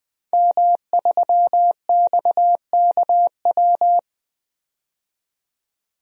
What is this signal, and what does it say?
Morse code, heard as a keyed tone, about 21 wpm, 700 Hz. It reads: M3XKW
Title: DXCC-100 Call Signs - 5 Characters - 20wpm